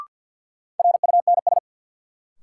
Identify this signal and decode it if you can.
Morse code, heard as a keyed tone, about 52 wpm, 700 Hz. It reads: RUNS
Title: Sentences from Top 600 Words - New Word Drill - 50wpm